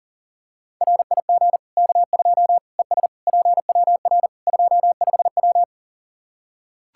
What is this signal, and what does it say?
Morse code, heard as a keyed tone, about 37 wpm, 700 Hz. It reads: RIG K2 ES PWR 25W
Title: Practice QSO - Exchange 2 - Info - 40wpm